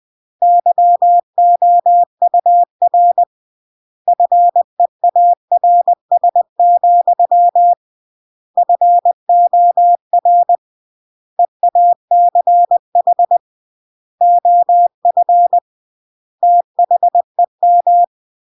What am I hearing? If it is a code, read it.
Morse code, heard as a keyed tone, about 20 wpm, 700 Hz. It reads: YOUR FEARS, FOR EACH OF THEM
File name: CW / jodorowsky / 82 commandments 20-20